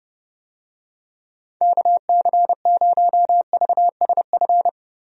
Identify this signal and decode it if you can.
Morse code, heard as a keyed tone, about 30 wpm, 700 Hz. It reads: KC0VSF
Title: USA Call Signs - 6 Characters - Repeated 2x - 30wpm